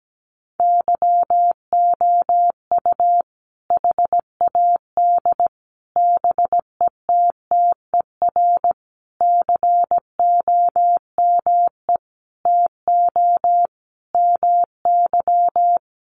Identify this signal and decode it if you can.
Morse code, heard as a keyed tone, about 17 wpm, 700 Hz. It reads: YOU HAD BETTER COME TO MY
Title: Sentences from Top 400 Words 17wpm